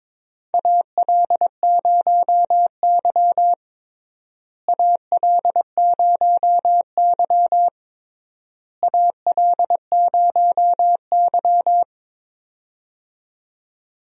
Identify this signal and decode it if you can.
Morse code, heard as a keyed tone, about 22 wpm, 700 Hz. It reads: AL0Y AL0Y AL0Y
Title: USA Call Signs - 4 Characters - Repeated 3x - 22wpm